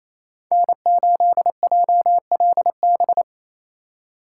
Morse code, heard as a keyed tone, about 37 wpm, 700 Hz. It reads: N8JLB